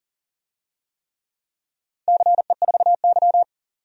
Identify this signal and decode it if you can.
Morse code, heard as a keyed tone, about 40 wpm, 700 Hz. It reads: CE4Y
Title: DXCC-100 Call Signs - 4 Characters - 40wpm